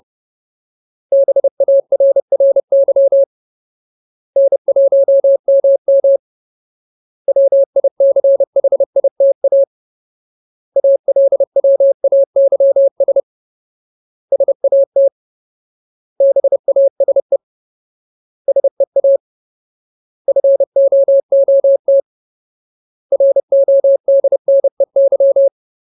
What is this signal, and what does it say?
Morse code, heard as a keyed tone, about 30 wpm, 550 Hz. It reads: BARRY N1MM WICHITA ALWAYS SAT BASE SEA FOOT RODNEY
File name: Ultimate Word List / 30WPM Ultimate